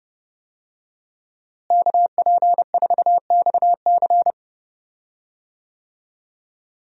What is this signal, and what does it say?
Morse code, heard as a keyed tone, about 30 wpm, 700 Hz. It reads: KP4XC
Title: DXCC-100 Call Signs - 5 Characters - Repeated 3x - 30wpm